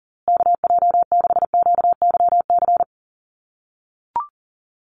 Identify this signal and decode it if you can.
Morse code, heard as a keyed tone, about 40 wpm, 700 Hz. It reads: KJ6QYC
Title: USA Call Signs Repeated 2x 40wpm